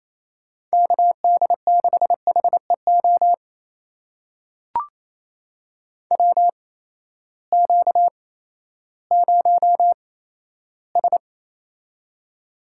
Morse code, heard as a keyed tone, about 28 wpm, spaced as for 14 wpm, 700 Hz. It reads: KD6HEO WQ0S
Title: USA Call Signs - 8x Character Spacing - 28wpm